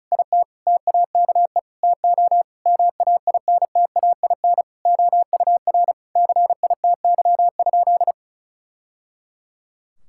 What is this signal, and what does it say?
Morse code, heard as a keyed tone, about 34 wpm, 700 Hz. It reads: IT TAKE TO MAINTAIN OUR CITY?
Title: Sentences from Top 1000 Words - Review Part 5 - 35wpm